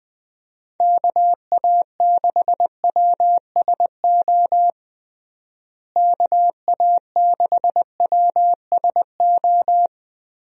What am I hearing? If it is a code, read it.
Morse code, heard as a keyed tone, about 20 wpm, 700 Hz. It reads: KA6WSO KA6WSO
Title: USA Call Signs - 6 Characters - Repeated 2x - 20wpm